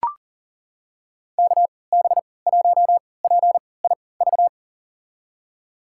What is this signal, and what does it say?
Morse code, heard as a keyed tone, about 40 wpm, spaced as for 23 wpm, 700 Hz. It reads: KB1PIV
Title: USA Call Signs - 6 Characters - 3x Character Spacing - 40wpm